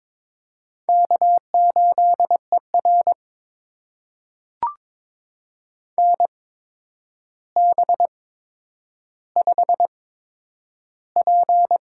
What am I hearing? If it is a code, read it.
Morse code, heard as a keyed tone, about 22 wpm, spaced as for 9 wpm, 700 Hz. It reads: K8ER NB5P
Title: USA Call Signs - 4 Characters - 8x Character Spacing - 22wpm